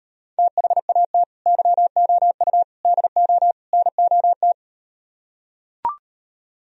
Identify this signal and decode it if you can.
Morse code, heard as a keyed tone, about 38 wpm, 700 Hz. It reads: THAT YOU DO NOT